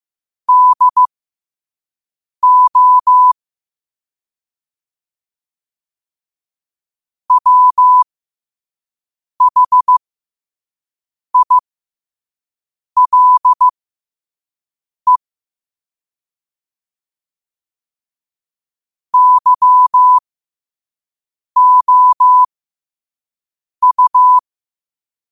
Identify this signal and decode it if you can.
Morse code, heard as a keyed tone, about 15 wpm, spaced as for 5 wpm, 1000 Hz. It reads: DO WHILE YOU
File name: qotdfiles / 5wpm